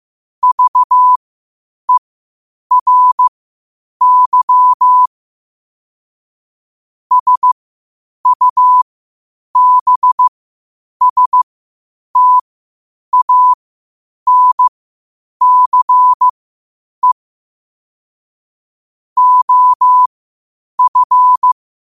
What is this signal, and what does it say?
Morse code, heard as a keyed tone, about 15 wpm, spaced as for 8 wpm, 1000 Hz. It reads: VERY SUBSTANCE OF
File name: qotdfiles / 8wpm